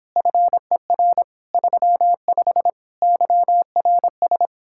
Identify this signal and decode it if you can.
Morse code, heard as a keyed tone, about 29 wpm, 700 Hz. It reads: FER 35 YRS